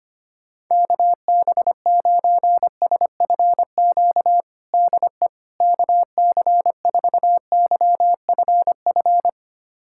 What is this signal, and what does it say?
Morse code, heard as a keyed tone, about 25 wpm, 700 Hz. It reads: KB9SFQ DE KC4YFF
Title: USA Callsign DE Callsign - 6 Characters - 25wpm